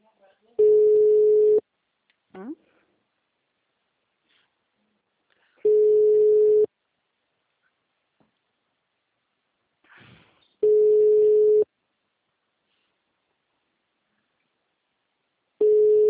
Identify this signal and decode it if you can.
Ringback tone (Europe, 425 Hz), with speech.